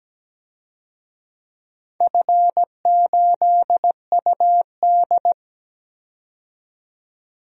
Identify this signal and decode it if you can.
Morse code, heard as a keyed tone, about 17 wpm, 700 Hz. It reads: F8UD
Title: DXCC-100 Call Signs - 4 Characters - 17wpm